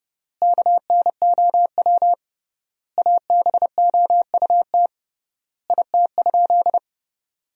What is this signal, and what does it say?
Morse code, heard as a keyed tone, about 30 wpm, 700 Hz. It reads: KNOW ABOUT IT?
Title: 3x Word-Spacing - Sentences from Top 100 Words - 30wpm